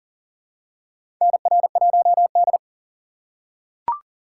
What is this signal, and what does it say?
Morse code, heard as a keyed tone, about 40 wpm, 700 Hz. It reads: NR1D
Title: USA Call Signs - 4 Characters - 3x Character Spacing - 40wpm